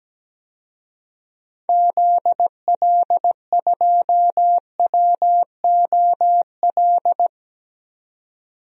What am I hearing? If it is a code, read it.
Morse code, heard as a keyed tone, about 17 wpm, 700 Hz. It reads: ZL2WOL
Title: DXCC-100 Call Signs - 6 Characters - 17wpm